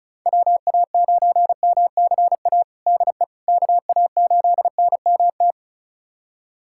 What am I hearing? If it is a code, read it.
Morse code, heard as a keyed tone, about 36 wpm, 700 Hz. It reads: WA9MCA DE KA8NMT
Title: USA Callsign DE Callsign - 6 Characters - 35wpm